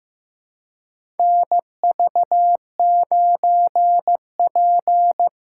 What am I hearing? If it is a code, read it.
Morse code, heard as a keyed tone, about 15 wpm, 700 Hz. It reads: NV9P
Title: USA Call Signs - 4 Characters - 2x Character Spacing - 15wpm